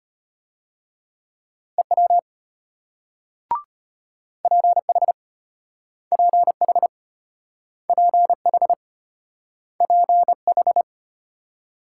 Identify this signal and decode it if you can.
Morse code, heard as a keyed tone, about 34 wpm, 700 Hz. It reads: EW PH PH PH PH